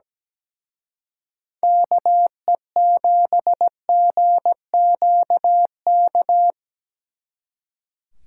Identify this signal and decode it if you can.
Morse code, heard as a keyed tone, about 20 wpm, 700 Hz. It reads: KE7GQK
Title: Speed Racing - USA Call Signs 17wpm